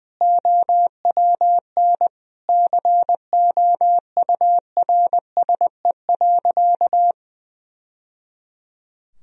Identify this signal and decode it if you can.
Morse code, heard as a keyed tone, about 20 wpm, 700 Hz. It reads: OWN COURSE.
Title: Sentences from Top 1000 Words - Review Part 1 - 20wpm